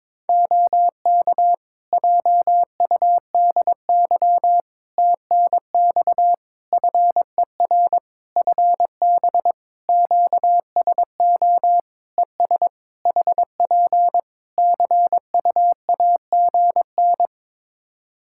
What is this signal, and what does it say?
Morse code, heard as a keyed tone, about 22 wpm, 700 Hz. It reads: OK JUDY TNX FER FB QSO ES HP CUAGN
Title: Practice QSO - Exchange 4 - Ending - 22wpm